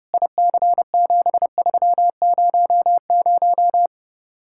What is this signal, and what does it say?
Morse code, heard as a keyed tone, about 32 wpm, 700 Hz. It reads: IC7300